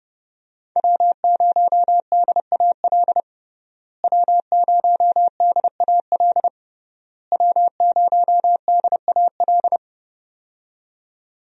Morse code, heard as a keyed tone, about 30 wpm, 700 Hz. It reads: W0DAL W0DAL W0DAL
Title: USA Call Signs - 5 Characters - Repeated 3x - 30wpm